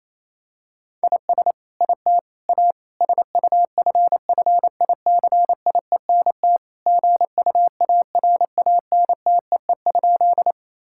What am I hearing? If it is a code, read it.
Morse code, heard as a keyed tone, about 28 wpm, 700 Hz. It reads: IS IT A SUFFICIENT GUARANTEE?
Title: Sentences from Top 1600 Words 28wpm